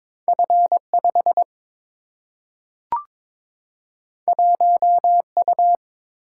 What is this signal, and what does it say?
Morse code, heard as a keyed tone, about 22 wpm, 700 Hz. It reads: F 5 1 U
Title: Single-Letter Single-Number 22wpm